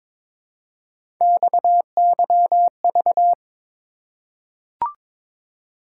Morse code, heard as a keyed tone, about 22 wpm, 700 Hz. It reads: XYV
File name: practice / 22wpm / Three Letter-Number Combinations - 22wpm